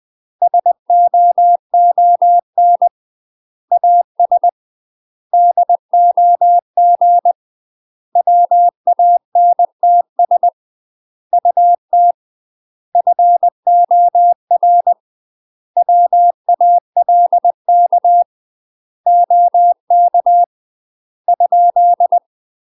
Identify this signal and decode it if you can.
Morse code, heard as a keyed tone, about 20 wpm, 700 Hz. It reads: SOON AS DOG WANTS UT FOR WALK OK ?
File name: UK MASTER QSOs TRAINING / 20WPM / UK MASTER QSOs 0074